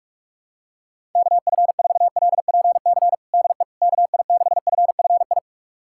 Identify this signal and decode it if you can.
Morse code, heard as a keyed tone, about 41 wpm, 700 Hz. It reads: KF4LPC DE KI6FFI
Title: USA Callsign DE Callsign - 6 Characters - 45wpm